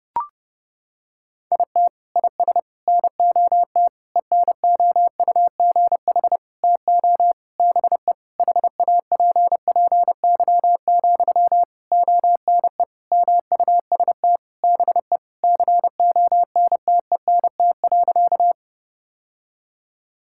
Morse code, heard as a keyed tone, about 32 wpm, 700 Hz. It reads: IT IS NOT ENOUGH TO BE HAPPY, ONE MUST BE CONTENT.
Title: Sentences from Top 1000 Words - Review Part 6 - 30wpm